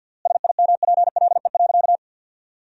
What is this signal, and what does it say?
Morse code, heard as a keyed tone, about 52 wpm, 700 Hz. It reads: SIMPLE.
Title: 3x Word-Spacing - Sentences from Top 500 Words - 50wpm